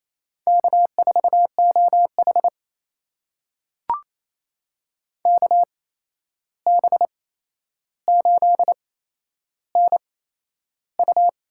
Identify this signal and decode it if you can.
Morse code, heard as a keyed tone, about 28 wpm, spaced as for 11 wpm, 700 Hz. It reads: K4OH KB8NU
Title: USA Call Signs - 8x Character Spacing - 28wpm